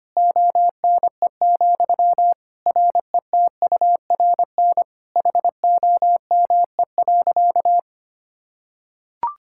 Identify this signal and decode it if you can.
Morse code, heard as a keyed tone, about 25 wpm, 700 Hz. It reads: ONE, RETURN HOME.
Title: Sentences from Top 1000 Words - Review Part 5 - 25wpm